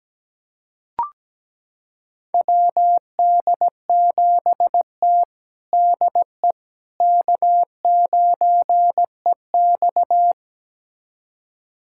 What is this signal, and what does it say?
Morse code, heard as a keyed tone, about 17 wpm, 700 Hz. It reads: WD7T DE K9EX
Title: USA Callsign DE Callsign - 4 Characters - 17wpm